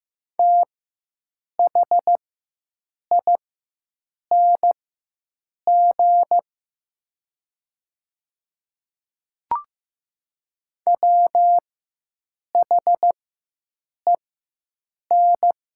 Morse code, heard as a keyed tone, about 15 wpm, spaced as for 7 wpm, 700 Hz. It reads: THING WHEN
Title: 4x Character Spacing - Top 300 Words - 15wpm